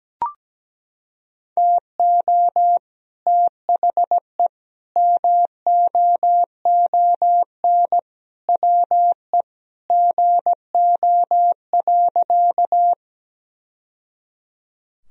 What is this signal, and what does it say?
Morse code, heard as a keyed tone, about 17 wpm, 700 Hz. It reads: TO THE MOON WE GO.
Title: Sentences from Top 1000 Words - Review Part 3 - 17wpm